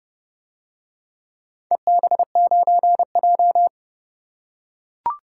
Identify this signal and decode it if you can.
Morse code, heard as a keyed tone, about 28 wpm, 700 Hz. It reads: EB9J